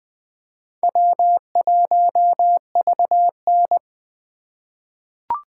Morse code, heard as a keyed tone, about 20 wpm, 700 Hz. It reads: W1VN
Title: USA Call Signs - 3x Character Spacing - 20wpm